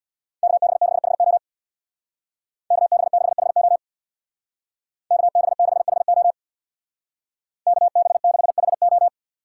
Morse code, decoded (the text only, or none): KB6HO KB6HO KB6HO KB6HO